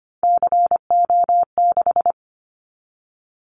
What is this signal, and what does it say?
Morse code, heard as a keyed tone, about 25 wpm, 700 Hz. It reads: CO6
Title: Callsign Prefixes for Top 100 DXCC - 25wpm